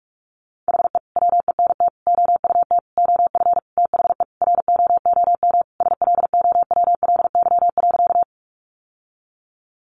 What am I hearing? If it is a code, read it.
Morse code, heard as a keyed tone, about 44 wpm, 700 Hz. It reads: HE WENT OUT OF THE ROOM SLOWLY.
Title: Sentences from Top 1100 Words 45wpm